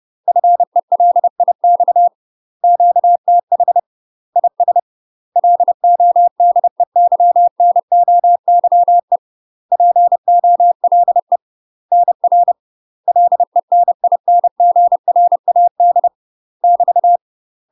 Morse code, heard as a keyed tone, about 30 wpm, 700 Hz. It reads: FELIX QTH IS LODEYNOYE POLE NR LENINGRAD <BT>
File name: UK MASTER QSOs TRAINING / 30WPM / UK MASTER QSOs 0038